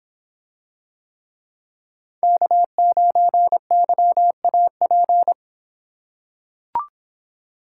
Morse code, heard as a keyed tone, about 26 wpm, 700 Hz. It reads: K9YAP